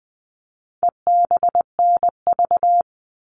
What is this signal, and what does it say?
Morse code, heard as a keyed tone, about 20 wpm, 700 Hz. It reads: EBNV
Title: All Cut Numbers - 4-Digit 20wpm